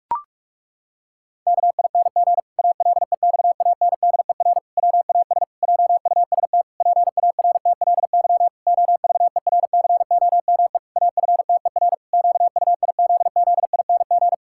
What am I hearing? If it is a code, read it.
Morse code, heard as a keyed tone, about 46 wpm, 700 Hz. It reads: KING ALEXANDER WAS JUST PARTLY OVERCOME AFTER QUIZZING
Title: Pangram Challenge v2 - 45wpm